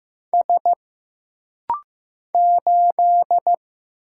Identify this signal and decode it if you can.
Morse code, heard as a keyed tone, about 15 wpm, 700 Hz. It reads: S 8
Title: Single Letter-Number Alt - Rapid-Fire 15wpm